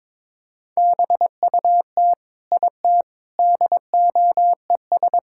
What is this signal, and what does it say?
Morse code, heard as a keyed tone, about 22 wpm, 700 Hz. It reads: BUT IT DOES
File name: practice / 22wpm / Sentences with 8-Word Phrases - 22wpm